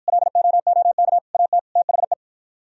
Morse code, heard as a keyed tone, about 53 wpm, 700 Hz. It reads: LOOK AT THE